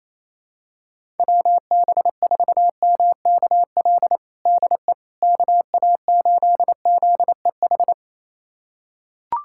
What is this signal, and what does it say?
Morse code, heard as a keyed tone, about 28 wpm, 700 Hz. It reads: WB4MKL DE KA8ZEH
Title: USA Callsign DE Callsign - 6 Characters - 28wpm